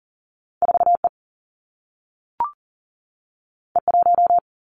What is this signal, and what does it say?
Morse code, heard as a keyed tone, about 39 wpm, 700 Hz. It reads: 4 E E 1
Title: Two Challenging Letter-Number Combinations 40wpm